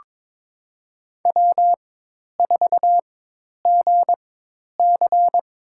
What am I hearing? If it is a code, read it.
Morse code, heard as a keyed tone, about 23 wpm, spaced as for 10 wpm, 700 Hz. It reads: W4GC
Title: USA Call Signs - 4x Character Spacing - 22wpm